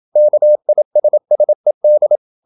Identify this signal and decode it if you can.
Morse code, heard as a keyed tone, about 27 wpm, 600 Hz. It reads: KISSED
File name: Never 0027wpm